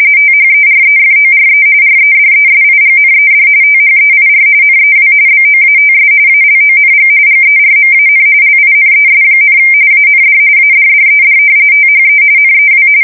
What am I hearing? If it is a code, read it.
RTTY, about 45.45 baud. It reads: THEREFORE REDESIGNED THEIR NEW BOATS ALONG GERMAN LINES, AND ADDED THE NEW
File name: fsk170 / usnavy4